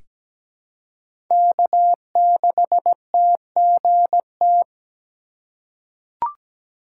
Morse code, heard as a keyed tone, about 17 wpm, 700 Hz. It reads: K6TGT
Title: USA Call Signs - 5 Characters - 2x Character Spacing - 17wpm